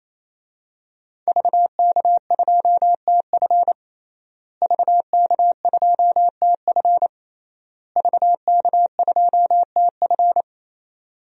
Morse code, heard as a keyed tone, about 28 wpm, 700 Hz. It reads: VK2TF VK2TF VK2TF
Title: DXCC-100 Call Signs - 5 Characters - Repeated 3x - 28wpm